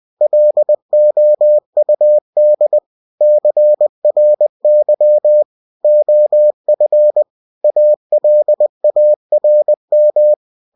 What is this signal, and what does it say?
Morse code, heard as a keyed tone, about 20 wpm, 600 Hz. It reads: LOUD CRY OF ALARM
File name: cw-books / winnie-the-pooh / winnie-20wpm-page-0085